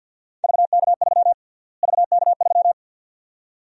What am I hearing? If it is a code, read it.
Morse code, heard as a keyed tone, about 50 wpm, 700 Hz. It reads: VK3 VK3
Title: USA Call Sign Prefixes - Repeated 2x - 50wpm